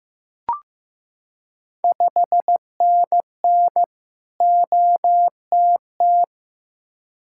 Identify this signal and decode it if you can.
Morse code, heard as a keyed tone, about 15 wpm, 700 Hz. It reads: 5NN OTT
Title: Ontario QSO Party Exchange 15wpm